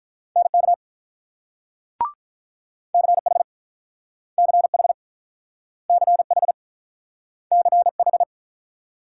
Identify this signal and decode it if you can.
Morse code, heard as a keyed tone, about 48 wpm, 700 Hz. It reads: NK CH CH CH CH